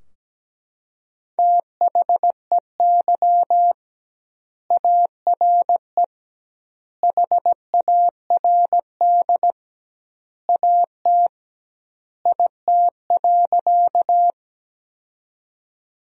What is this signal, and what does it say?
Morse code, heard as a keyed tone, about 17 wpm, 700 Hz. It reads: THEY ARE HARD AT IT.
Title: 2x Word-Spacing - Sentences from Top 300 Words - 17wpm